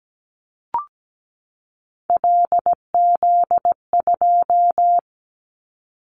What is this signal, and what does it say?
Morse code, heard as a keyed tone, about 17 wpm, 700 Hz. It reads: LZ2
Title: Callsign Prefixes for Top 100 DXCC - 17wpm